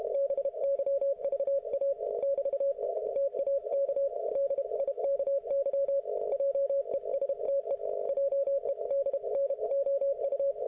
Morse code, heard as a keyed tone, about 32 wpm, 550 Hz. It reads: BYVA <BT> TAK DIKY JESTE JEDNOU